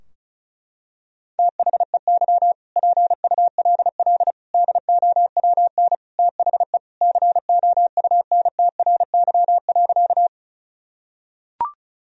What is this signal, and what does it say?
Morse code, heard as a keyed tone, about 36 wpm, 700 Hz. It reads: THEY PULL DOWN THE COUNTRY.
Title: Sentences from Top 1000 Words - Review Part 2 - 35wpm